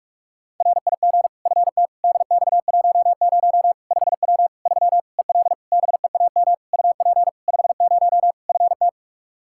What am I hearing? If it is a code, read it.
Morse code, heard as a keyed tone, about 46 wpm, 700 Hz. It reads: RIG FT DX10 5W 3 EL BEAM UP 50 FT
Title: Practice QSO - Exchange 2 - Info - 45wpm